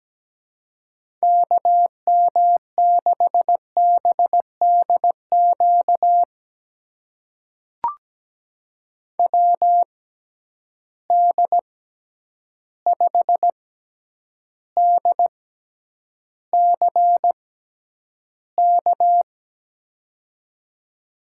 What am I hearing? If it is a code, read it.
Morse code, heard as a keyed tone, about 17 wpm, spaced as for 9 wpm, 700 Hz. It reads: KM6BDQ WD5DCK
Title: USA Call Signs - 6 Characters - 6x Character Spacing - 17wpm